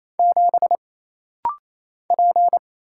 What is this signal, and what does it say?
Morse code, heard as a keyed tone, about 27 wpm, 700 Hz. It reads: 7 P
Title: Single Letter-Number Alt - Rapid-Fire 28wpm